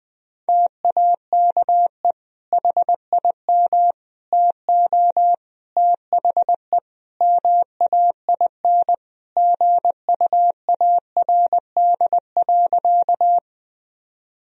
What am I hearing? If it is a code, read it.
Morse code, heard as a keyed tone, about 20 wpm, 700 Hz. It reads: TAKE HIM TO THE MAIN GUARD.
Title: Sentences from Top 1000 Words - Review Part 4 - 20wpm